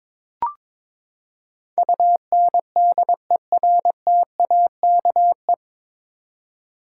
Morse code, heard as a keyed tone, about 22 wpm, 700 Hz. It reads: UNDERTAKE
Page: Morse Code Ninja practice 22wpm